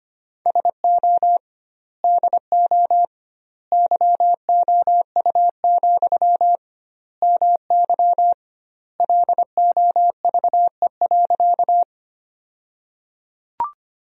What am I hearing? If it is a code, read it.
Morse code, heard as a keyed tone, about 25 wpm, 700 Hz. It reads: SO DO YOU, MY LOVE.
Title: 2x Word-Spacing - Sentences from Top 400 Words - 25wpm